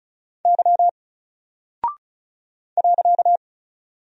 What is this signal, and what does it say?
Morse code, heard as a keyed tone, about 36 wpm, 700 Hz. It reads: Y .
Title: Single Character (Letter:Number:Punctuation:Prosign) Alt 35wpm